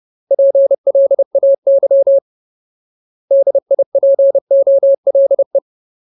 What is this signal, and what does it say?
Morse code, heard as a keyed tone, about 30 wpm, 550 Hz. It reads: PLAY DIPOLE
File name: Ultimate Word List / 30WPM Ultimate